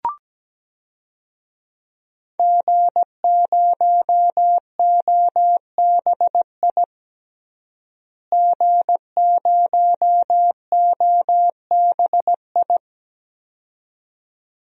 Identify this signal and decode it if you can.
Morse code, heard as a keyed tone, about 17 wpm, 700 Hz. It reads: G0OBI G0OBI
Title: DXCC-100 Call Signs - 5 Characters - Repeated 2x - 17wpm